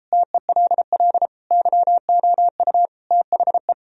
Morse code, heard as a keyed tone, about 37 wpm, 700 Hz. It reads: TELL YOU THE